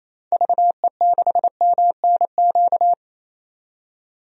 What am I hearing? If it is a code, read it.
Morse code, heard as a keyed tone, about 28 wpm, 700 Hz. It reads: VE6MNQ
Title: DXCC-100 Call Signs - 6 Characters - 28wpm